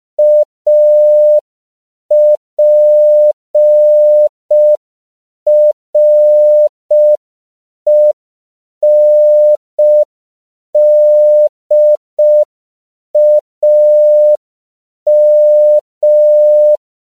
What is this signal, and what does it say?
Morse code, heard as a keyed tone, about 5 wpm, 600 Hz. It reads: APRENDAM
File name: CW / 5PPM